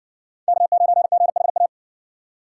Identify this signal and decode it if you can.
Morse code, heard as a keyed tone, about 53 wpm, 700 Hz. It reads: K9GHA